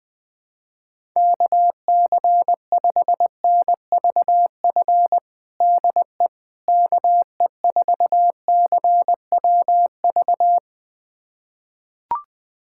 Morse code, heard as a keyed tone, about 20 wpm, 700 Hz. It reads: KC5NVF DE KE4CWV
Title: USA Callsign DE Callsign 20wpm